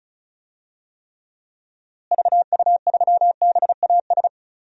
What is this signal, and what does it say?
Morse code, heard as a keyed tone, about 36 wpm, 700 Hz. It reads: VU3BAS
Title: DXCC-100 Call Signs - 6 Characters - Repeated 2x - 35wpm